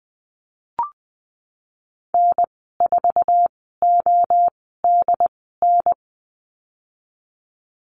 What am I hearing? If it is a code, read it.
Morse code, heard as a keyed tone, about 20 wpm, spaced as for 14 wpm, 700 Hz. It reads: N4ODN
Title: 069 Lesson 12g - Letter d - Call Signs 20wpm